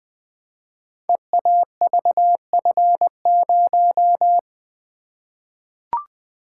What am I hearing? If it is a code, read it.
Morse code, heard as a keyed tone, about 20 wpm, 700 Hz. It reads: EAVF0
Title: Code Groups - 5 Letter:Numbers - 20wpm